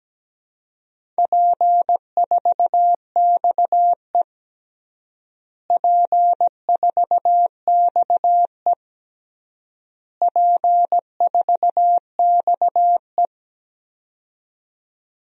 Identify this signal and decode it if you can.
Morse code, heard as a keyed tone, about 17 wpm, 700 Hz. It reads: P4XE P4XE P4XE
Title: DXCC-100 Call Signs - 4 Characters - Repeated 3x - 17wpm